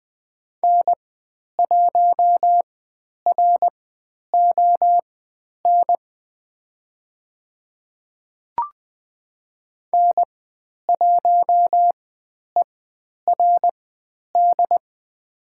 Morse code, heard as a keyed tone, about 20 wpm, spaced as for 10 wpm, 700 Hz. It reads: N1RON N1ERD
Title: 109 Lesson 18g - Letter m - Call Signs 20wpm